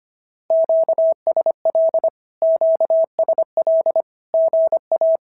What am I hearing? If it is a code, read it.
Morse code, heard as a keyed tone, about 25 wpm, 650 Hz. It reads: QSL QSL GA
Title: SOTA QSO Practice 25wpm